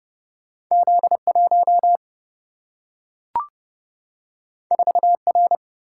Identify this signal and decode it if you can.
Morse code, heard as a keyed tone, about 30 wpm, 700 Hz. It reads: Z1 4R
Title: Single-Letter Single-Number 30wpm